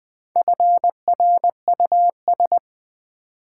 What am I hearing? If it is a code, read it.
Morse code, heard as a keyed tone, about 20 wpm, 700 Hz. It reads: FRUS
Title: Build A Word - 4 Letters At A Time - 11-Letter Words - 20wpm